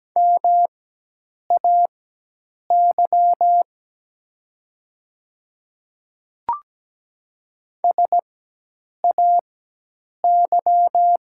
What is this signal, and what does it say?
Morse code, heard as a keyed tone, about 17 wpm, spaced as for 8 wpm, 700 Hz. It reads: MAY SAY
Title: 4x Character Spacing - Top 500 Words - 17wpm